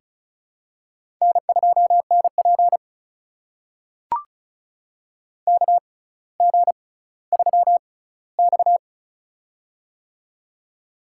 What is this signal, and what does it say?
Morse code, heard as a keyed tone, about 36 wpm, spaced as for 18 wpm, 700 Hz. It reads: N2NP KG3X
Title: USA Call Signs - 4 Characters - 6x Character Spacing - 35wpm